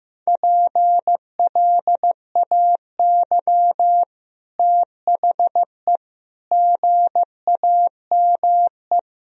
Morse code, heard as a keyed tone, about 15 wpm, 700 Hz. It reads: PLAY THE GAME
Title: Sentences from Top 1000 Words - Review Part 3 - 15wpm